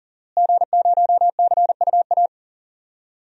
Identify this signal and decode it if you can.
Morse code, heard as a keyed tone, about 40 wpm, 700 Hz. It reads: G0CUA
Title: DXCC-100 Call Signs - 5 Characters - Repeated 3x - 40wpm